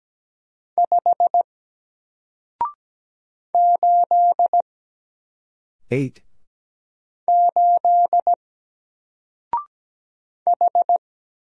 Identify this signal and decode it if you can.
Morse code, heard as a keyed tone, about 17 wpm, 700 Hz. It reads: 5 8 8 H
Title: Single Character (Letter:Number:Punctuation:Prosign) 17wpm